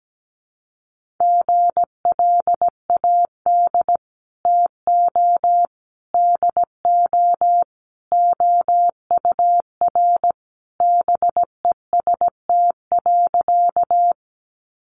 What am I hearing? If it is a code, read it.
Morse code, heard as a keyed tone, about 17 wpm, 700 Hz. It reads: GLAD TO DO OUR BEST.
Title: Sentences from Top 800 Words 17wpm